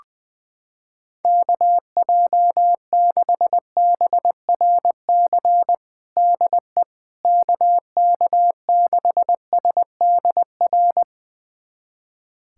Morse code, heard as a keyed tone, about 20 wpm, 700 Hz. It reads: KJ6BRC DE KK6SDR
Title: USA Callsign DE Callsign - 6 Characters - 20wpm